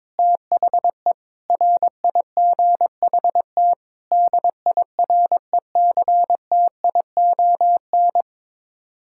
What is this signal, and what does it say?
Morse code, heard as a keyed tone, about 22 wpm, 700 Hz. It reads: THE RIGHT DIRECTION
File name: practice / 22wpm / Sets of 8 Words - 22wpm